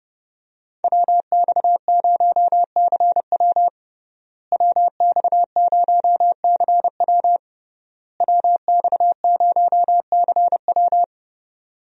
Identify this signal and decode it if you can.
Morse code, heard as a keyed tone, about 30 wpm, 700 Hz. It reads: WX0CW WX0CW WX0CW
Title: USA Call Signs - 5 Characters - Repeated 3x - 30wpm